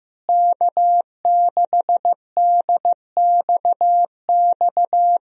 Morse code, heard as a keyed tone, about 16 wpm, 700 Hz. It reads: K6DXX
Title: USA Call Signs - 5 Characters - Repeated 3x - 15wpm